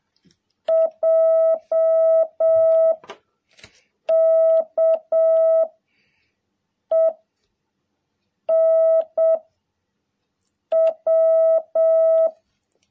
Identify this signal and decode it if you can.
Morse code, heard as a keyed tone, about 7 wpm, 660 Hz. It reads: J K E N W
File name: jh8cbh / 2021cw